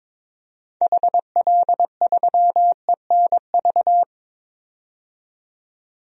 Morse code, heard as a keyed tone, about 21 wpm, 700 Hz. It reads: HL3ENV
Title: DXCC-100 Call Signs - 6 Characters - 22wpm